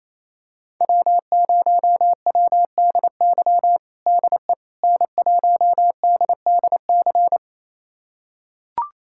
Morse code, heard as a keyed tone, about 28 wpm, 700 Hz. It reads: W0WDY DE N1DDC
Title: USA Callsign DE Callsign - 5 Characters - 28wpm